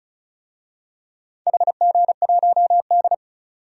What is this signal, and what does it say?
Morse code, heard as a keyed tone, about 36 wpm, 700 Hz. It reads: HG1D
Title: DXCC-100 Call Signs - 4 Characters - 35wpm